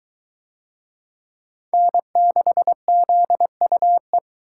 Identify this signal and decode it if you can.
Morse code, heard as a keyed tone, about 21 wpm, 700 Hz. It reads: N6ZUE